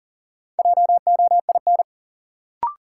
Morse code, heard as a keyed tone, about 37 wpm, 700 Hz. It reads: JOIN
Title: Sentences from Top 1000 Words - Review Part 6 - New Word Drill - 40wpm